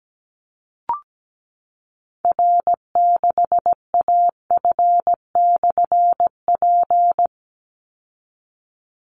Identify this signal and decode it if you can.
Morse code, heard as a keyed tone, about 17 wpm, 700 Hz. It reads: R6AF/P
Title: Call Signs Repeated 2x - 17wpm